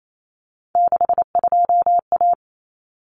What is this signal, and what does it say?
Morse code, heard as a keyed tone, about 28 wpm, 700 Hz. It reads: 62A
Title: Common Cut Numbers - 3-Digit 28wpm